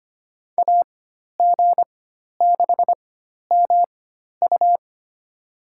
Morse code, heard as a keyed tone, about 25 wpm, spaced as for 12 wpm, 700 Hz. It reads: AG6MU
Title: USA Call Signs - 5 Characters - 4x Character Spacing - 25wpm